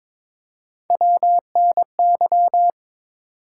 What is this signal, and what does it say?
Morse code, heard as a keyed tone, about 22 wpm, 700 Hz. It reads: WNY
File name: ARRL Sections v2 - 22wpm